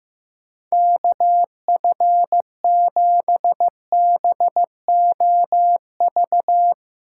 Morse code, heard as a keyed tone, about 15 wpm, 700 Hz. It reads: KF7BOV
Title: USA Callsign DE Callsign 15wpm